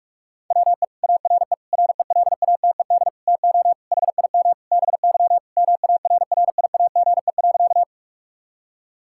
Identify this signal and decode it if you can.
Morse code, heard as a keyed tone, about 46 wpm, 700 Hz. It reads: WE ARE RELATED TO HIM BY MARRIAGE.
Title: Sentences from Top 1000 Words - Review Part 5 - 45wpm